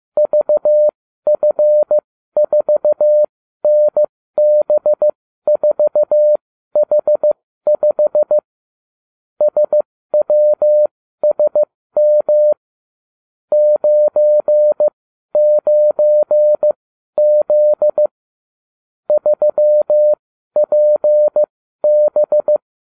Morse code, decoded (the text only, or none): VF4NB4H5 SWSM 99Z 3PB